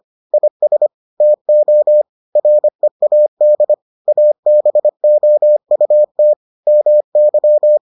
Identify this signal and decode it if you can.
Morse code, heard as a keyed tone, about 25 wpm, 600 Hz. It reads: IS TO READ ABOUT MY